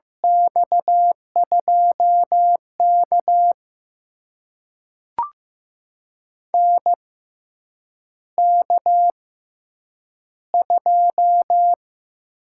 Morse code, heard as a keyed tone, about 15 wpm, spaced as for 8 wpm, 700 Hz. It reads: X2K NK2
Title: USA Call Signs - 4 Characters - 6x Character Spacing - 15wpm